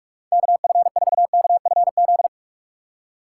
Morse code, heard as a keyed tone, about 44 wpm, 700 Hz. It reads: KF4KFZ